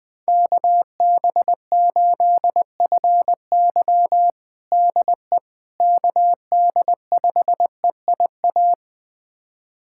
Morse code, heard as a keyed tone, about 21 wpm, 700 Hz. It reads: KB8FY DE KD5EIA
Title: USA Callsign DE Callsign 20wpm